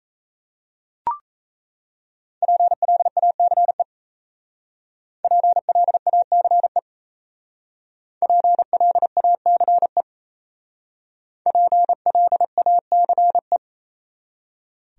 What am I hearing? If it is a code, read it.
Morse code, heard as a keyed tone, about 33 wpm, 700 Hz. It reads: PLACE PLACE PLACE PLACE
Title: Speed-Racing Top 500 Words 28wpm